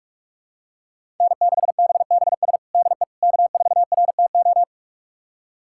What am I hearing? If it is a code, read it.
Morse code, heard as a keyed tone, about 46 wpm, 700 Hz. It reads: N6BBS DE K4RTO
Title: USA Callsign DE Callsign - 5 Characters - 45wpm